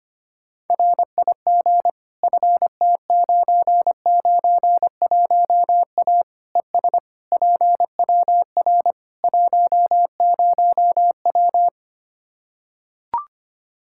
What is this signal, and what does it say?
Morse code, heard as a keyed tone, about 25 wpm, 700 Hz. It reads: RIG FT991A ES PWR 10W
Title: Practice QSO - Exchange 2 - Info - 25wpm